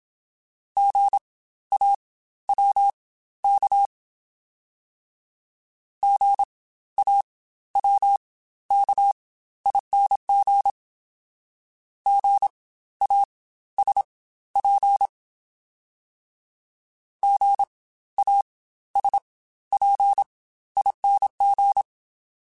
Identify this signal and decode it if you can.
Morse code, heard as a keyed tone, about 27 wpm, spaced as for 14 wpm, 800 Hz. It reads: GAWK GAWKING GASP GASPING
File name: Spaced Alphas / Spaced Alphas mp3s / Gs sp @25